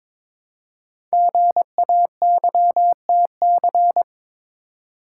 Morse code, heard as a keyed tone, about 22 wpm, 700 Hz. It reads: GAYTC
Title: Code Groups - 5 Letters - 22wpm